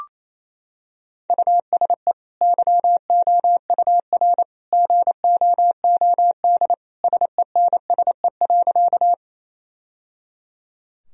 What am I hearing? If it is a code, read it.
Morse code, heard as a keyed tone, about 27 wpm, 700 Hz. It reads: USE YOUR GOOD SENSE.
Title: Sentences from Top 1000 Words - Review Part 4 - 28wpm